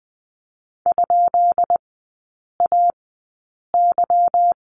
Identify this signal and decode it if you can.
Morse code, heard as a keyed tone, about 20 wpm, spaced as for 8 wpm, 700 Hz. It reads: ?AY
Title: 145 Lesson 24e - Letter y - Three Characters 20wpm